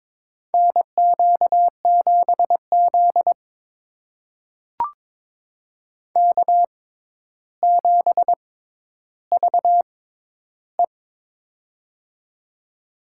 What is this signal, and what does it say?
Morse code, heard as a keyed tone, about 23 wpm, spaced as for 11 wpm, 700 Hz. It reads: NQ7Z K7VE
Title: USA Call Signs - 4 Characters - 6x Character Spacing - 22wpm